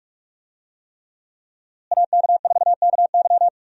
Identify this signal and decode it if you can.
Morse code, heard as a keyed tone, about 45 wpm, 700 Hz. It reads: AK4KY
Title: USA Call Signs - 5 Characters - Repeated 3x - 45wpm